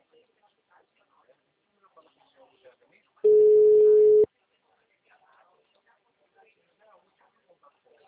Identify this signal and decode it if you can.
Ringback tone (Europe, 425 Hz), with speech.